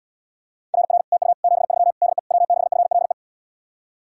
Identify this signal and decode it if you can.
Morse code, heard as a keyed tone, about 80 wpm, 700 Hz. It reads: RR TU 73 DE W6FLE